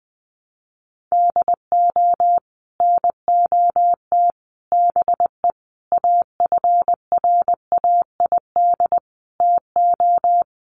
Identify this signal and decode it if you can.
Morse code, heard as a keyed tone, about 20 wpm, 700 Hz. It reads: DO NOT BE AFRAID TO
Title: Sets of 5 Words Encore2 20wpm